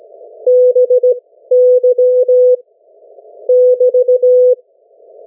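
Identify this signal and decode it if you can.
Morse code, heard as a keyed tone, about 17 wpm, spaced as for 13 wpm, 500 Hz. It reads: BY <BT>
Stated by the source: Morse code from the national amateur radio CW net of New Zealand, which meets on 3535.0 kHz or 7030 kHz.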